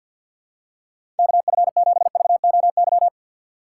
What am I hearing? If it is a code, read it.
Morse code, heard as a keyed tone, about 51 wpm, 700 Hz. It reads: KF7VOY
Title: USA Call Signs - 2x Character Spacing - 50wpm